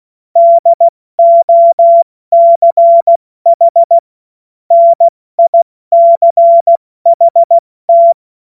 Morse code, heard as a keyed tone, about 16 wpm, 680 Hz. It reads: DOCH NICHT
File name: Morsekurs / Witze / Witz 01. März 2023 70 F80